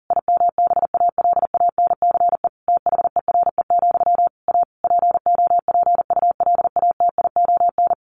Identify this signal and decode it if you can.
Morse code, heard as a keyed tone, about 40 wpm, 700 Hz. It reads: IMBALANCE THERE, A POPULATION
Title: A Sound Of Thunder 40wpm